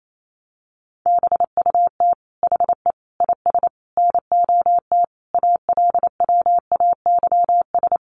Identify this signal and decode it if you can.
Morse code, heard as a keyed tone, about 28 wpm, 700 Hz. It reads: BUT HE IS NOT ALWAYS
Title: Sentences from Top 1000 Words 28wpm